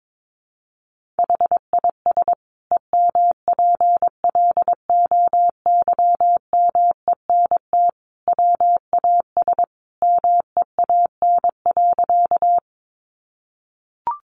Morse code, heard as a keyed tone, about 22 wpm, 700 Hz. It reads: HIS EMPLOYMENT WAS MEAN.
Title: Sentences from Top 1500 Words 22wpm